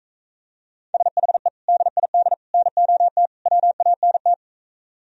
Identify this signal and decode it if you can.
Morse code, heard as a keyed tone, about 45 wpm, 700 Hz. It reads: SHE DID NOT WANT